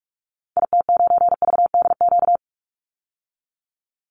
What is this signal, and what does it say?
Morse code, heard as a keyed tone, about 44 wpm, 700 Hz. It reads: IT9VDQ
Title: Call Signs - 45wpm